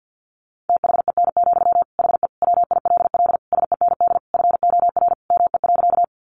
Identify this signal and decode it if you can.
Morse code, heard as a keyed tone, about 48 wpm, 700 Hz. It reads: THEN, HE WILL SEND FOR ME.
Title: Sentences from Top 1500 Words 50wpm